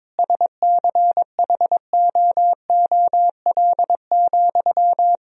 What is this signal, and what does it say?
Morse code, heard as a keyed tone, about 22 wpm, 700 Hz. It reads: SCHOOL,
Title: Sentences from Top 1000 Words - Review Part 3 - 22wpm